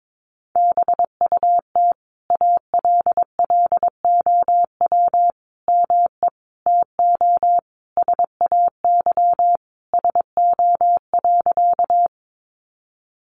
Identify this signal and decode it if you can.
Morse code, heard as a keyed tone, about 22 wpm, 700 Hz. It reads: BUT ALLOW ME TO SAY SO.
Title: Sentences from Top 700 Words 22wpm